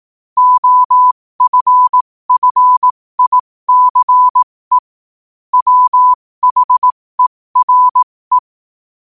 Morse code, heard as a keyed tone, about 18 wpm, spaced as for 15 wpm, 1000 Hz. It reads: OFFICE WHERE
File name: qotdfiles / 15wpm